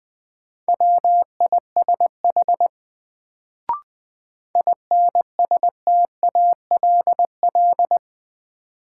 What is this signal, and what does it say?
Morse code, heard as a keyed tone, about 20 wpm, 700 Hz. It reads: WISH INSTALL
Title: Sentences from Top 1600 Words - New Word Drill Alt - 20wpm